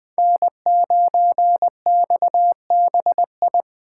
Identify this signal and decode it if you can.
Morse code, heard as a keyed tone, about 20 wpm, 700 Hz. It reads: N9XBI
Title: Speed Racing - USA 1x3 Call Signs 17wpm